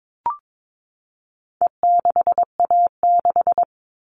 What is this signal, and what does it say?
Morse code, heard as a keyed tone, about 22 wpm, 700 Hz. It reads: E6A6
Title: All Cut Numbers - 4-Digit 22wpm